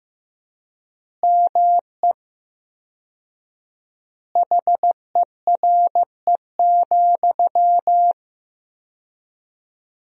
Morse code, heard as a keyed tone, about 16 wpm, 700 Hz. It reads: ME HERE,
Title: 4x Word-Spacing - Sentences from Top 500 Words - 15wpm